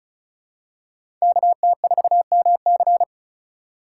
Morse code, heard as a keyed tone, about 36 wpm, 700 Hz. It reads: KT4MC
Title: USA Call Signs - 5 Characters - 4x Character Spacing - 35wpm